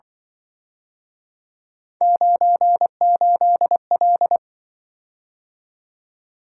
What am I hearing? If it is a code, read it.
Morse code, heard as a keyed tone, about 23 wpm, 700 Hz. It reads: 98L
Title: Speed-Racing - Three Letter-Number Combinations - 20wpm